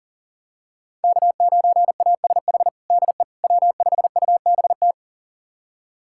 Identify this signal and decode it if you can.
Morse code, heard as a keyed tone, about 37 wpm, 700 Hz. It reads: K9ASH DE W5UBT
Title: USA Callsign DE Callsign - 5 Characters - 40wpm